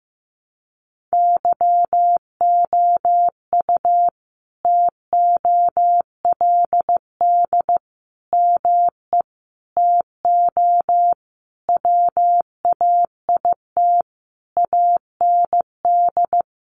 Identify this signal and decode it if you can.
Morse code, heard as a keyed tone, about 15 wpm, 700 Hz. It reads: YOU TOLD ME TO WAIT AND
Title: Sentences from Top 1000 Words 15wpm